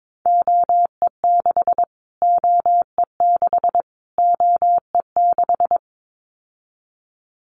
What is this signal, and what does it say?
Morse code, heard as a keyed tone, about 22 wpm, 700 Hz. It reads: OE6 OE6 OE6
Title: Callsign Prefixes for Top 25 DXCC - Repeated 3x - 22wpm